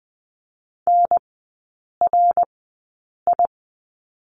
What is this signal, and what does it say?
Morse code, heard as a keyed tone, about 20 wpm, spaced as for 8 wpm, 700 Hz. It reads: NRI
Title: 053 Lesson 10e - Letter r - Three Characters 20wpm